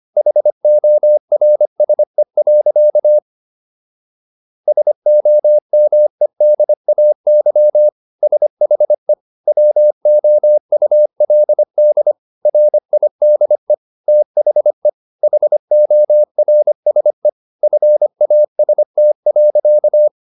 Morse code, heard as a keyed tone, about 25 wpm, 600 Hz. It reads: HORSE. SOMEDAY SHE WOULD RIDE THE HORSE FAST.
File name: SS123 25